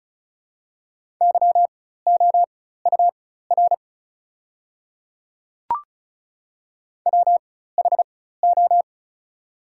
Morse code, heard as a keyed tone, about 34 wpm, spaced as for 16 wpm, 700 Hz. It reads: YOUR WHO
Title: 4x Character Spacing - Top 100 Words - 35wpm